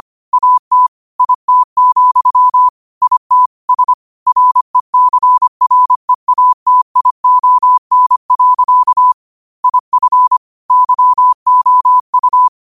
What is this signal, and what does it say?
Morse code, heard as a keyed tone, about 25 wpm, 1000 Hz. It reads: AT IT, IT S RECREATION. IF YOU